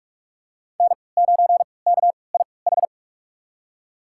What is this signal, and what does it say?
Morse code, heard as a keyed tone, about 44 wpm, spaced as for 25 wpm, 700 Hz. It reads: N9KIH
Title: USA Call Signs - 5 Characters - 3x Character Spacing - 45wpm